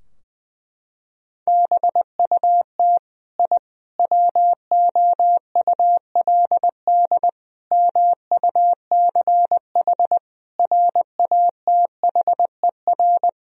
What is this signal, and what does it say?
Morse code, heard as a keyed tone, about 20 wpm, 700 Hz. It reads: BUT I WOULD MUCH RATHER
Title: Sentences from Top 1000 Words - Review Part 4 - 20wpm